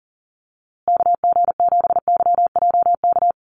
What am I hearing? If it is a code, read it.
Morse code, heard as a keyed tone, about 40 wpm, 700 Hz. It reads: KG7YJK
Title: USA Call Signs Repeated 3x 40wpm